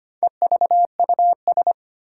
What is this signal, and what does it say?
Morse code, heard as a keyed tone, about 25 wpm, 700 Hz. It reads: EVUS
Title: Four Challenging Letter-Number Combinations 25wpm